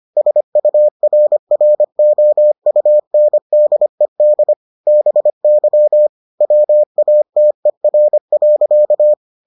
Morse code, heard as a keyed tone, about 25 wpm, 600 Hz. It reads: SURROUNDED BY WATER.